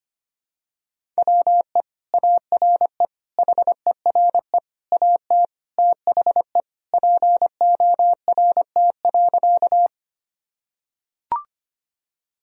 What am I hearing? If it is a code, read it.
Morse code, heard as a keyed tone, about 25 wpm, 700 Hz. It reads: WE ARE HERE AT THE PORT.
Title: Sentences from Top 1000 Words - Review Part 1 - 25wpm